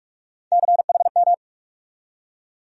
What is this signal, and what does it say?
Morse code, heard as a keyed tone, about 46 wpm, 700 Hz. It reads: CHM
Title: USA Call Sign Postfixes - 3 Letters 45wpm